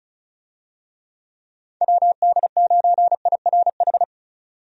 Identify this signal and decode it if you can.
Morse code, heard as a keyed tone, about 36 wpm, 700 Hz. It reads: WD9IRH
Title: USA Call Signs - 6 Characters - 35wpm